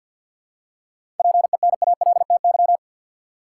Morse code, heard as a keyed tone, about 50 wpm, 700 Hz. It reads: PENALTY
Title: Seven Letter Words - 50wpm